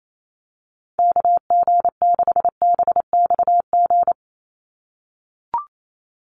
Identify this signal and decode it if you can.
Morse code, heard as a keyed tone, about 28 wpm, 700 Hz. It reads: KG6BXG
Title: USA Call Signs 28wpm